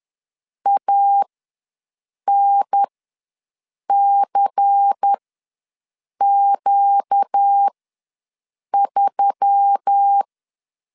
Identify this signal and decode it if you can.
Morse code, heard as a keyed tone, about 10 wpm, spaced as for 6 wpm, 800 Hz. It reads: ANCQ3
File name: LEZIONI / 5 LBVC / 5 LBVC ws 10wpm 3733